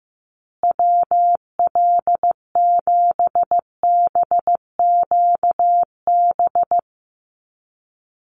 Morse code, heard as a keyed tone, about 15 wpm, 700 Hz. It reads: WL7BQB
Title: USA Call Signs Repeated 3x 15wpm